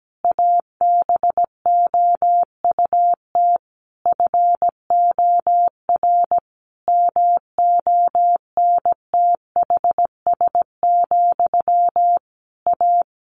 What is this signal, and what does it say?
Morse code, heard as a keyed tone, about 17 wpm, 700 Hz. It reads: ABOUT FOR MONTHS, A
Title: A Sound Of Thunder 17wpm